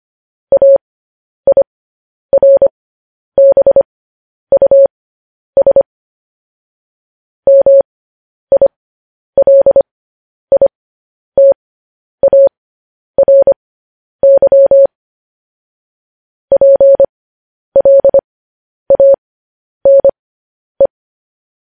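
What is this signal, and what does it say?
Morse code, heard as a keyed tone, about 25 wpm, spaced as for 10 wpm, 550 Hz. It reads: AIRBUS MILITARY PLANE
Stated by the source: Morse code news headlines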